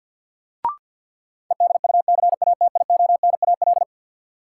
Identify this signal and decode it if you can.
Morse code, heard as a keyed tone, about 53 wpm, 700 Hz. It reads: EDUCATIONAL
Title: Sentences from Top 1200 Words - New Word Drill - 50wpm